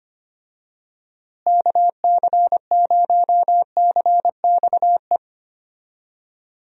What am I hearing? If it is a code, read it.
Morse code, heard as a keyed tone, about 25 wpm, 700 Hz. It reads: KC0CXE
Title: USA Call Signs - 6 Characters - Repeated 2x - 25wpm